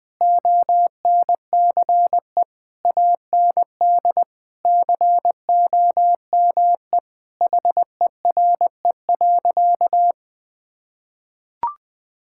Morse code, heard as a keyed tone, about 20 wpm, 700 Hz. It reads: ONCE AND COME HERE.
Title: Sentences from Top 1600 Words 20wpm